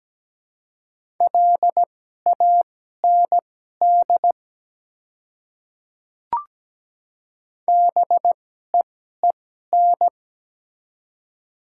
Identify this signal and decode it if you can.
Morse code, heard as a keyed tone, about 17 wpm, spaced as for 12 wpm, 700 Hz. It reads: LAND BEEN
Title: 2x Character Spacing - Top 200 Words - 17wpm